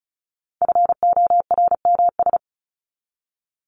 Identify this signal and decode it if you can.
Morse code, heard as a keyed tone, about 34 wpm, 700 Hz. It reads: FORMS
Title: ES:S Suffix - Top 100 Words with Base Words 35wpm